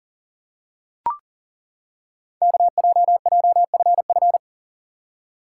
Morse code, heard as a keyed tone, about 41 wpm, 690 Hz. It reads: KJJFF
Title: Code Groups - 5 Letters - 40wpm